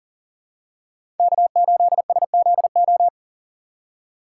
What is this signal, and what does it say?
Morse code, heard as a keyed tone, about 40 wpm, 700 Hz. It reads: K8SZO
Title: USA Call Signs - 5 Characters - 4x Character Spacing - 40wpm